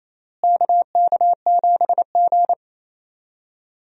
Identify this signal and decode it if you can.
Morse code, heard as a keyed tone, about 28 wpm, 700 Hz. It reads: KK7G
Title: USA Call Signs - 4 Characters - Repeated 3x - 28wpm